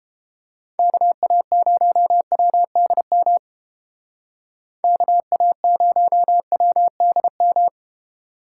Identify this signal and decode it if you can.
Morse code, heard as a keyed tone, about 31 wpm, 700 Hz. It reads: KA0WDM KA0WDM